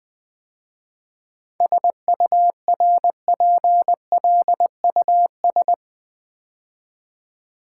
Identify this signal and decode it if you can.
Morse code, heard as a keyed tone, about 20 wpm, 700 Hz. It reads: SURPLUS
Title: 7 Letter Words - First 3 Letters Given - 20wpm